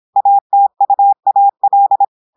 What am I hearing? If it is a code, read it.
Morse code, heard as a keyed tone, about 26 wpm, 800 Hz. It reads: ATUAL
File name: audios / portugais / portugais texte en clair 1 vitesse 26 mots minutes